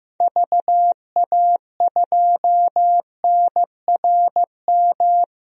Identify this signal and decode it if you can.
Morse code, heard as a keyed tone, about 15 wpm, 700 Hz. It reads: VA2NRM
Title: DXCC-100 Call Signs - 6 Characters - 15wpm